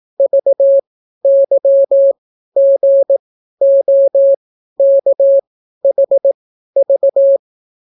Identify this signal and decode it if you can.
Morse code, heard as a keyed tone, about 18 wpm, spaced as for 12 wpm, 550 Hz. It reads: VYGOKHV